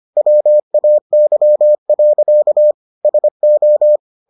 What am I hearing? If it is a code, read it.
Morse code, heard as a keyed tone, about 25 wpm, 600 Hz. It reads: WAY. SO